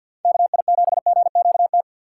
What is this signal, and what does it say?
Morse code, heard as a keyed tone, about 52 wpm, 700 Hz. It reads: KI7GQT
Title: USA Call Signs - 6 Characters - 3x Character Spacing - 50wpm